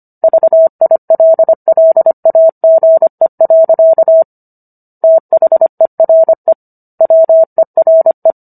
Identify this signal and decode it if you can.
Morse code, heard as a keyed tone, about 25 wpm, 650 Hz. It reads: VILLAGE. THERE WERE